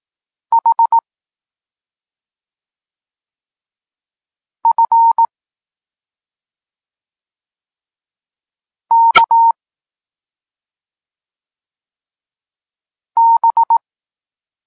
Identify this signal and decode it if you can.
Morse code, heard as a keyed tone, about 18 wpm, 920 Hz. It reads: H F K B